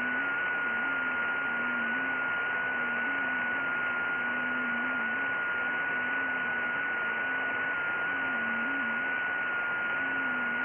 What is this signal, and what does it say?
FT8.
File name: IC-7100